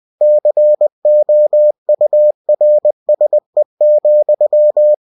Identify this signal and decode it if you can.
Morse code, heard as a keyed tone, about 20 wpm, 600 Hz. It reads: COURSE,
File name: cw-books / winnie-the-pooh / winnie-20wpm-page-0084